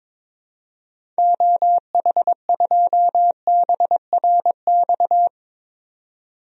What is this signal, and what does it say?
Morse code, heard as a keyed tone, about 22 wpm, 700 Hz. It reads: OH2BRX
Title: DXCC-100 Call Signs - 6 Characters - 22wpm